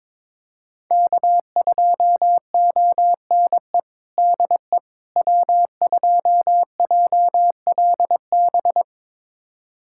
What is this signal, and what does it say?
Morse code, heard as a keyed tone, about 22 wpm, 700 Hz. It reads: K2ONE DE W2JLB
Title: USA Callsign DE Callsign - 5 Characters - 22wpm